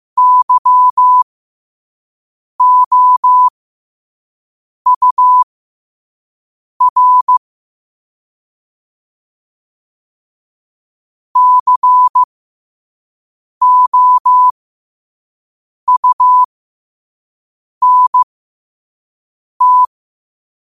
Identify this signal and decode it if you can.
Morse code, heard as a keyed tone, about 15 wpm, spaced as for 5 wpm, 1000 Hz. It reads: YOUR COUNT
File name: qotdfiles / 5wpm